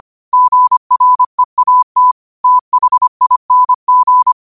Morse code, heard as a keyed tone, about 25 wpm, 1000 Hz. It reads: GREAT THING
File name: qotdfiles / 25wpm